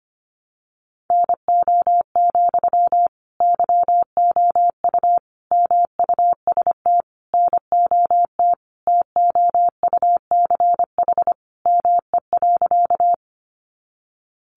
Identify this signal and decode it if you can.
Morse code, heard as a keyed tone, about 25 wpm, 700 Hz. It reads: NO, YOU MUST NOT TOUCH ME.
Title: Sentences from Top 1200 Words 25wpm